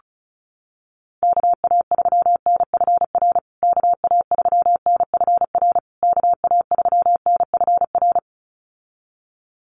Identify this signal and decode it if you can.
Morse code, heard as a keyed tone, about 34 wpm, 700 Hz. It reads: KA3NFR KA3NFR KA3NFR
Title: USA Call Signs Repeated 3x 35wpm